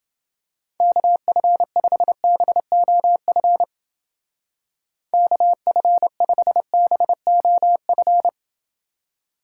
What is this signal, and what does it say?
Morse code, heard as a keyed tone, about 30 wpm, 700 Hz. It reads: KF5BOF KF5BOF